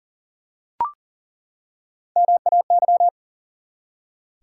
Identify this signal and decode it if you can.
Morse code, heard as a keyed tone, about 37 wpm, 700 Hz. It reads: MAY
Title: Three Letter Words - 40wpm